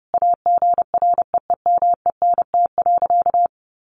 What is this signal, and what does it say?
Morse code, heard as a keyed tone, about 30 wpm, 700 Hz. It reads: AGREEMENT.
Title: Sentences from Top 1500 Words 30wpm